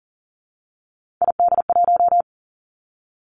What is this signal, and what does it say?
Morse code, heard as a keyed tone, about 41 wpm, 700 Hz. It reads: ID1
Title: Three Challenging Letter-Number Combinations 40wpm